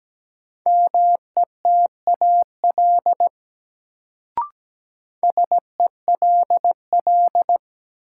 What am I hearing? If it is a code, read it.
Morse code, heard as a keyed tone, about 17 wpm, 700 Hz. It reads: METAL SELL
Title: Sentences from Top 1000 Words - Review Part 6 - New Word Drill - 17wpm